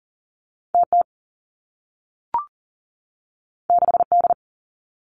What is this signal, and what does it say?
Morse code, heard as a keyed tone, about 40 wpm, 700 Hz. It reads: TT 6D
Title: Two Challenging Letter-Number Combinations 40wpm